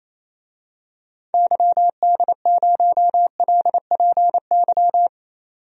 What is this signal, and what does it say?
Morse code, heard as a keyed tone, about 28 wpm, 700 Hz. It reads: YD0LPY
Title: DXCC-100 Call Signs - 6 Characters - 28wpm